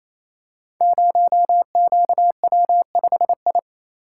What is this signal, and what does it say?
Morse code, heard as a keyed tone, about 27 wpm, 700 Hz. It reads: 0QW5I
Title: Code Groups - 5 Letter:Numbers - 28wpm